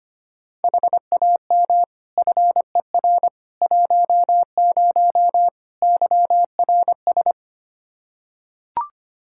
Morse code, heard as a keyed tone, about 25 wpm, 700 Hz. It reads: HAM FER 10 YRS
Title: Practice QSO - Exchange 3 - More Info - 25wpm